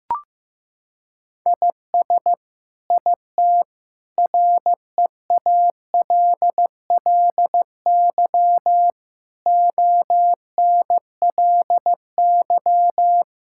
Morse code, heard as a keyed tone, about 16 wpm, 700 Hz. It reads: IS IT REALLY ONLY
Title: Sentences from Top 1000 Words - Review Part 6 - 15wpm